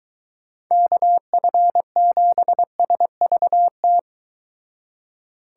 Morse code, heard as a keyed tone, about 23 wpm, 700 Hz. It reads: KF7SVT